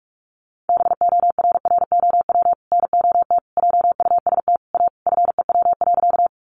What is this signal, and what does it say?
Morse code, heard as a keyed tone, about 44 wpm, 700 Hz. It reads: BORROW NOT JUST A FEW.
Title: Sentences from Top 1500 Words 45wpm